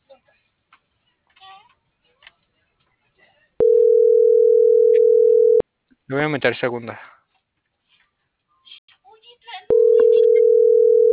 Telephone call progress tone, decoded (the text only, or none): ringback tone